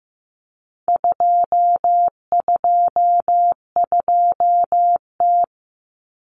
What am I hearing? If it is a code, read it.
Morse code, heard as a keyed tone, about 15 wpm, 700 Hz. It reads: 222T
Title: Common Cut Numbers - 4-Digit 15wpm